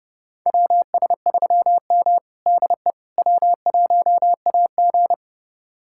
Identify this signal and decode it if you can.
Morse code, heard as a keyed tone, about 31 wpm, 700 Hz. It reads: WS3M DE W1AG
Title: USA Callsign DE Callsign - 4 Characters - 30wpm